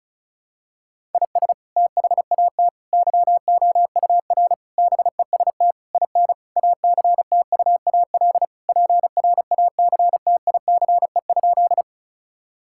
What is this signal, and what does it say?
Morse code, heard as a keyed tone, about 36 wpm, 700 Hz. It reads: IS THAT YOUR BEST IN ACTUAL PRACTICE?
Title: Sentences from Top 1000 Words - Review Part 5 - 35wpm